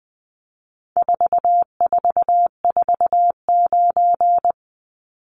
Morse code, heard as a keyed tone, about 20 wpm, 700 Hz. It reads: 4449
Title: Numbers - 4-Digit 20wpm